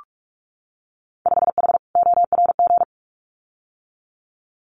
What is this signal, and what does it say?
Morse code, heard as a keyed tone, about 46 wpm, 700 Hz. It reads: 5H ORG
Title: Winter Field Day Exchanges Encore 45wpm